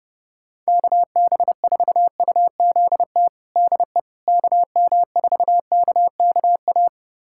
Morse code, heard as a keyed tone, about 32 wpm, 700 Hz. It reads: KB4UZT DE KM4KKA
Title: USA Callsign DE Callsign - 6 Characters - 30wpm